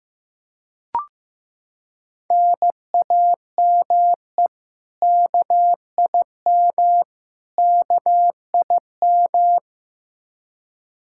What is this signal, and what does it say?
Morse code, heard as a keyed tone, about 16 wpm, 700 Hz. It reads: NAME KIM KIM
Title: K1USN SST QSO Practice 15wpm